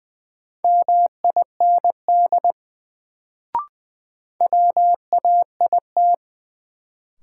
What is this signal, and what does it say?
Morse code, heard as a keyed tone, about 20 wpm, 700 Hz. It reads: MIND WAIT
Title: Sentences from Top 500 Words - New Word Drill - 20wpm